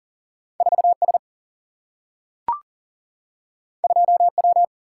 Morse code, heard as a keyed tone, about 39 wpm, 700 Hz. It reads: 4 S 2 W
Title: Single-Letter Single-Number 40wpm